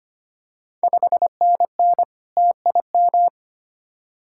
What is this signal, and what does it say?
Morse code, heard as a keyed tone, about 25 wpm, 690 Hz. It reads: 5NN TIM
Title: Ontario QSO Party Exchange 25wpm